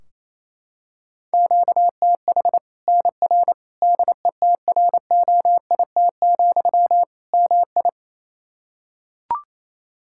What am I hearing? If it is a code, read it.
Morse code, heard as a keyed tone, about 28 wpm, 700 Hz. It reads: QTH NR DETROIT, MI
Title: K1USN SST QSO Practice 28wpm